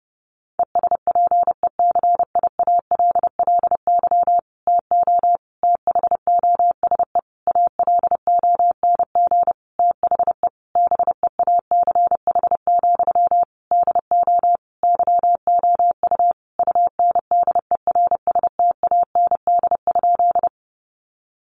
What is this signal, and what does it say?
Morse code, heard as a keyed tone, about 30 wpm, 700 Hz. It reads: ESPECIALLY TO THOSE ALONG THE BEACH, DO YOU UNDERSTAND?
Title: Sentences from Top 1500 Words 30wpm